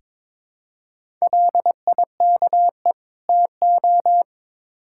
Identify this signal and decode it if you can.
Morse code, heard as a keyed tone, about 22 wpm, 700 Hz. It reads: LIKE TO
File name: Build A Phrase - 2-Word Phrases - 22wpm